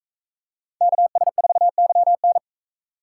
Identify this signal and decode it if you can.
Morse code, heard as a keyed tone, about 38 wpm, 700 Hz. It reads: KS4YN